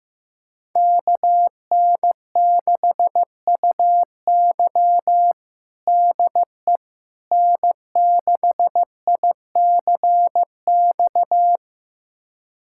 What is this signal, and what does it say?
Morse code, heard as a keyed tone, about 15 wpm, 700 Hz. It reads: KN6UY DE N6ICX
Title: USA Callsign DE Callsign - 5 Characters - 15wpm